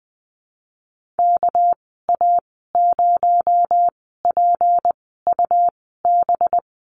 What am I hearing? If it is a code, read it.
Morse code, heard as a keyed tone, about 20 wpm, spaced as for 15 wpm, 700 Hz. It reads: KA0PUB
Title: 219 Lesson 35g - Number 0 - Call Signs 20wpm